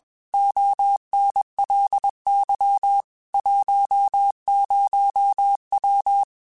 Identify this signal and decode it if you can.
Morse code, heard as a keyed tone, about 21 wpm, 800 Hz. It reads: ONLY 10W
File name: QSOs / QSOs-mp3s / kc7sac@20